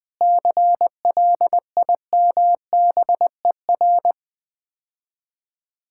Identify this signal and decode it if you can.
Morse code, heard as a keyed tone, about 20 wpm, 700 Hz. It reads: CLIMBER
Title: Difficult Words - Speed-Racing with Hints - 20wpm